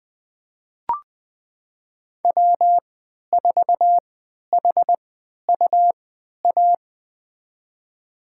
Morse code, heard as a keyed tone, about 21 wpm, spaced as for 12 wpm, 700 Hz. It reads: W4HUA
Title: USA Call Signs - 3x Character Spacing - 20wpm